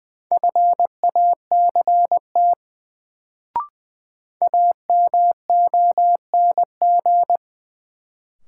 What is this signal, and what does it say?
Morse code, heard as a keyed tone, about 20 wpm, 700 Hz. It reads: FACT AMONG
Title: Sets of 2 Words - New Word Drill - 20wpm